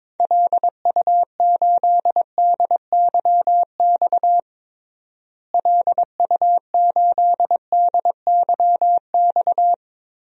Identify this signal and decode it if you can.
Morse code, heard as a keyed tone, about 22 wpm, 700 Hz. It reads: LU8DYX LU8DYX
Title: DXCC-100 Call Signs - 6 Characters - Repeated 3x - 22wpm